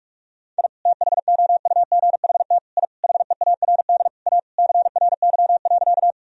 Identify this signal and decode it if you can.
Morse code, heard as a keyed tone, about 46 wpm, 700 Hz. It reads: I THOUGHT I HEARD A CRY.
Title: Sentences from Top 1000 Words - Review Part 2 - 45wpm